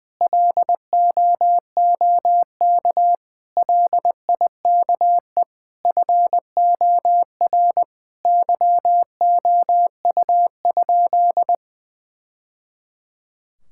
Morse code, heard as a keyed tone, about 20 wpm, 700 Hz. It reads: LOOK LIKE FOR YOU?
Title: Sentences from Top 1000 Words - Review Part 6 - 20wpm